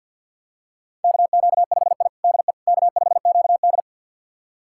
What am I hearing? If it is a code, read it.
Morse code, heard as a keyed tone, about 53 wpm, 700 Hz. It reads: KQ5I DE K5QD
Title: USA Callsign DE Callsign - 4 Characters - 50wpm